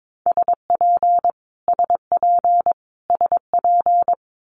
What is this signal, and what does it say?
Morse code, heard as a keyed tone, about 22 wpm, 700 Hz. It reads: SP SP SP
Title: Multi-letter Phonemes Repeated 3x 22wpm